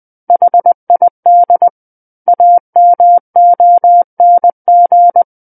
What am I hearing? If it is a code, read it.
Morse code, heard as a keyed tone, about 20 wpm, 700 Hz. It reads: HID AMONG
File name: 20 WPM